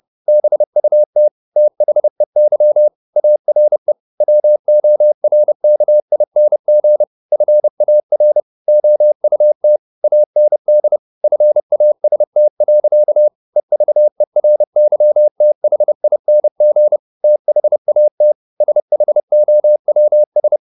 Morse code, heard as a keyed tone, about 30 wpm, 600 Hz. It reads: BUT THEY ARE WORKING FAR OUT AND FAST. EVERYTHING THAT SHOWS
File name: the old man and the sea / the old man and the sea 1 30wpm